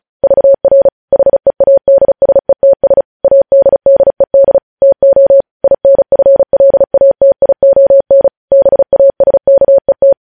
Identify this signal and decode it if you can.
Morse code, heard as a keyed tone, about 36 wpm, 550 Hz. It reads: VR HEADSETS ADDED TO INFLATION BASKET
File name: podcast / News headlines-1x-35x35